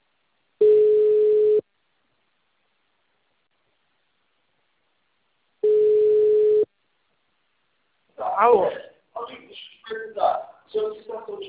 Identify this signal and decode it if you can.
Ringback tone (Europe, 425 Hz), with speech.
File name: out- -2007- - - .3832